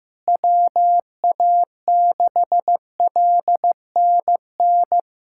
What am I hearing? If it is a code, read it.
Morse code, heard as a keyed tone, about 15 wpm, 700 Hz. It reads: WA6LNN
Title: USA Callsign DE Callsign 15wpm